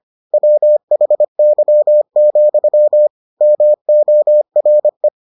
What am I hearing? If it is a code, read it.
Morse code, heard as a keyed tone, about 25 wpm, 600 Hz. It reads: WHY, MORE